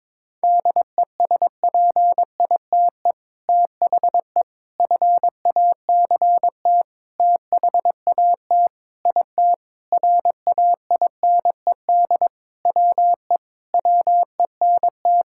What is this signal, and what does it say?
Morse code, heard as a keyed tone, about 22 wpm, 700 Hz. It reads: DESPITE THE FACT THAT IT RAINED WE WENT
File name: practice / 22wpm / Sentences with 4-Word Phrases - 22wpm